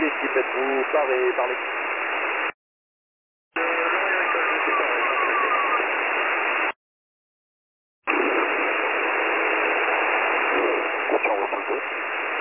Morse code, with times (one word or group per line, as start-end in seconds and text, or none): none